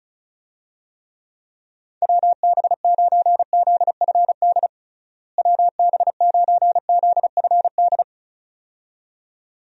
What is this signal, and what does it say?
Morse code, heard as a keyed tone, about 36 wpm, 700 Hz. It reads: WB9ZFD WB9ZFD
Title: USA Call Signs - 6 Characters - Repeated 2x - 35wpm